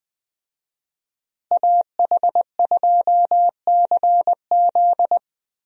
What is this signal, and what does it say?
Morse code, heard as a keyed tone, about 20 wpm, 700 Hz. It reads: AH2CZ
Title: USA Call Signs - 5 Characters - Repeated 3x - 20wpm